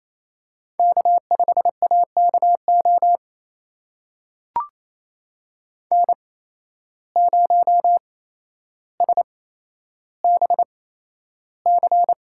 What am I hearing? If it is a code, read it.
Morse code, heard as a keyed tone, about 28 wpm, spaced as for 12 wpm, 700 Hz. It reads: K5AKO N0SBC
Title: USA Call Signs - 8x Character Spacing - 28wpm